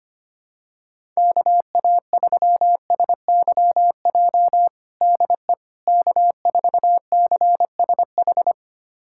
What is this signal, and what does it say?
Morse code, heard as a keyed tone, about 25 wpm, 700 Hz. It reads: KA3SYJ DE K4CSH
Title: USA Callsign DE Callsign 25wpm